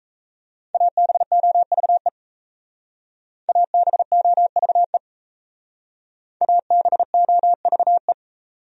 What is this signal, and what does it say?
Morse code, heard as a keyed tone, about 34 wpm, 700 Hz. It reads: ABOVE ABOVE ABOVE
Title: Speed-Racing Top 500 Words 28wpm